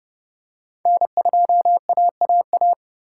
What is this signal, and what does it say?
Morse code, heard as a keyed tone, about 29 wpm, 700 Hz. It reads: N2AAA